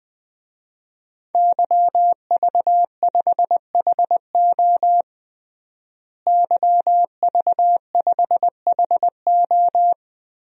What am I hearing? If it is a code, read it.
Morse code, heard as a keyed tone, about 20 wpm, 700 Hz. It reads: YV5HO YV5HO
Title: DXCC-100 Call Signs - 5 Characters - Repeated 2x - 20wpm